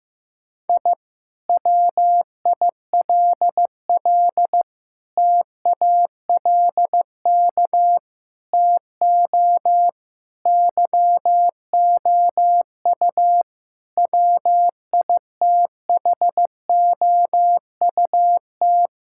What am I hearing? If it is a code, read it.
Morse code, heard as a keyed tone, about 15 wpm, 700 Hz. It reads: I WILL TALK TO YOU WITHOUT
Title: Sentences from Top 1000 Words - Review Part 6 - 15wpm